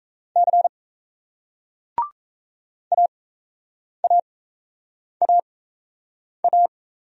Morse code, heard as a keyed tone, about 40 wpm, spaced as for 8 wpm, 700 Hz. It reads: C AAAA